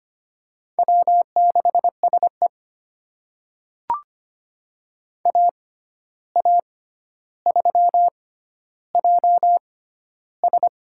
Morse code, heard as a keyed tone, about 25 wpm, spaced as for 12 wpm, 700 Hz. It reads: W6SE AA3JS
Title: USA Call Signs - 6x Character Spacing - 25wpm